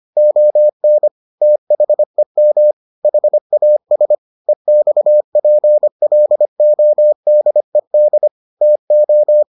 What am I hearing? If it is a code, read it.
Morse code, heard as a keyed tone, about 25 wpm, 600 Hz. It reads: ON THEM HAS EXPLODED TO